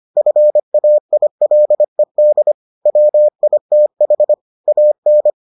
Morse code, heard as a keyed tone, about 25 wpm, 600 Hz. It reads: FAILED WITH AN